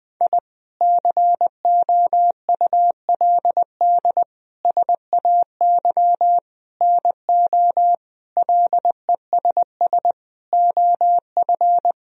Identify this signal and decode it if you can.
Morse code, heard as a keyed tone, about 20 wpm, 700 Hz. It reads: I COULD SAY NO LESS OF
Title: Sentences from Top 1000 Words - Review Part 2 - 20wpm